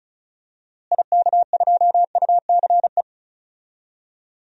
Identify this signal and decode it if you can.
Morse code, heard as a keyed tone, about 36 wpm, 700 Hz. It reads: IK2UCE
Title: DXCC-100 Call Signs - 6 Characters - Repeated 3x - 35wpm